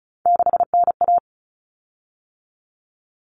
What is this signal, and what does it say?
Morse code, heard as a keyed tone, about 34 wpm, 700 Hz. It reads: 6NA
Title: Common Cut Numbers - 3-Digit 35wpm